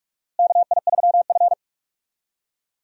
Morse code, heard as a keyed tone, about 46 wpm, 700 Hz. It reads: KI3F
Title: USA Call Signs - 4 Characters - Repeated 2x - 45wpm